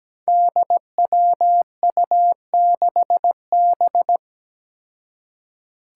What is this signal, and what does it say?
Morse code, heard as a keyed tone, about 17 wpm, 700 Hz. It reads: DWU6B
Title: Five Challenging Letter-Number Combinations 17wpm